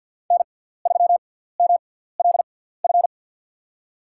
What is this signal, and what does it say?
Morse code, heard as a keyed tone, about 47 wpm, spaced as for 17 wpm, 700 Hz. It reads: N3MLF
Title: USA Call Signs - 5 Characters - 6x Character Spacing - 50wpm